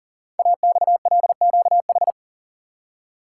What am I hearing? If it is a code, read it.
Morse code, heard as a keyed tone, about 42 wpm, 700 Hz. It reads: AXLQH
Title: Code Groups - 5 Letters - 40wpm